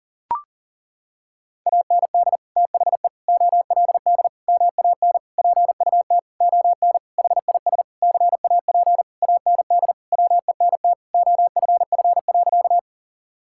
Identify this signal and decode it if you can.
Morse code, heard as a keyed tone, about 40 wpm, 700 Hz. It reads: AND THE OLD MAN PUT ON HIS CAP AND WENT OFF.
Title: Sentences from Top 1600 Words 40wpm